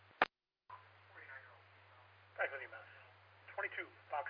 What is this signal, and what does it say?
Emergency Alert System (SAME) tones, heard as harsh buzzing bursts, with speech. It is heard from 1.1 to 3.9 s.